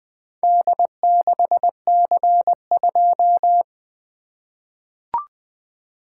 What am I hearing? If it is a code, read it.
Morse code, heard as a keyed tone, about 20 wpm, 700 Hz. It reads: D6C2
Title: Four Letter-Number Combinations - 20wpm